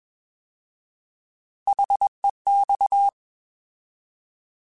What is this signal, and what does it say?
Morse code, heard as a keyed tone, about 21 wpm, 800 Hz. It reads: HEX
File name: Alphabetical / Alphabetical mp3s / Hs @20